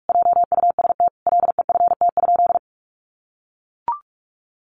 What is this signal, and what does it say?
Morse code, heard as a keyed tone, about 44 wpm, 700 Hz. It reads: JUST LEFT?
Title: Sentences from Top 1200 Words 45wpm